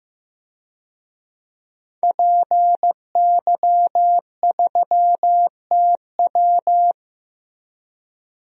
Morse code, heard as a keyed tone, about 15 wpm, 700 Hz. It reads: PY3TW
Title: DXCC-100 Call Signs - 5 Characters - 15wpm